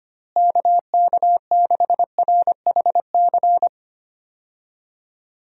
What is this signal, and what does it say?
Morse code, heard as a keyed tone, about 25 wpm, 700 Hz. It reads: KK6RHC
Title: USA Call Signs - 6 Characters - Repeated 3x - 25wpm